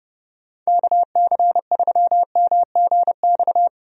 Morse code, heard as a keyed tone, about 30 wpm, 700 Hz. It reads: KC3MGX
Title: USA Call Signs - 6 Characters - 30wpm